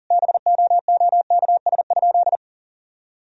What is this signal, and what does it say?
Morse code, heard as a keyed tone, about 40 wpm, 700 Hz. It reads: BOOKS?
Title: Sentences from Top 1000 Words - Review Part 4 - 40wpm